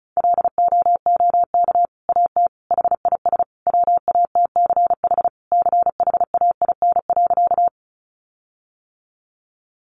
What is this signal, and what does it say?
Morse code, heard as a keyed tone, about 34 wpm, 700 Hz. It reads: LOOK AT HIS WATCH CHAIN.
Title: Sentences from Top 1500 Words 35wpm